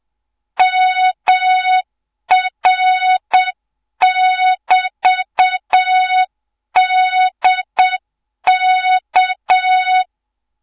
Morse code, heard as a keyed tone, about 7 wpm, 3700 Hz, with noise. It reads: MR<BT>DK
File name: Morze-HA2MN / hang / 24 GYAKORLAT